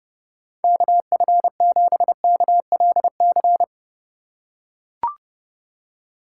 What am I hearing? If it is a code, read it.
Morse code, heard as a keyed tone, about 30 wpm, 700 Hz. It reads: KF7KLC